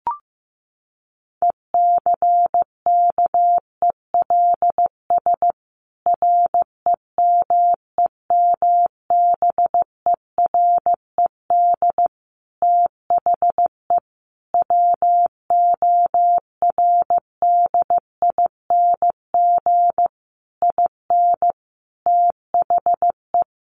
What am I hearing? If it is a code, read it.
Morse code, heard as a keyed tone, about 15 wpm, 700 Hz. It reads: ECKELS REMEMBERED THE WORDING IN THE
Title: A Sound Of Thunder 15wpm